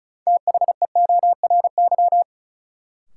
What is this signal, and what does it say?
Morse code, heard as a keyed tone, about 36 wpm, 700 Hz. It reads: THEORY
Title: Sentences from Top 600 Words - New Word Drill - 35wpm